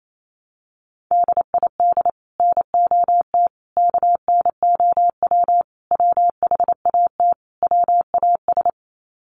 Morse code, heard as a keyed tone, about 28 wpm, 700 Hz. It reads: DID NOT KNOW WHAT WAS
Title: Sets of 5 Words Encore2 28wpm